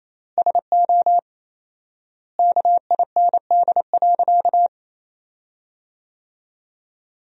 Morse code, heard as a keyed tone, about 28 wpm, 700 Hz. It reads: SO KIND.
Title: 4x Word-Spacing - Sentences from Top 500 Words - 28wpm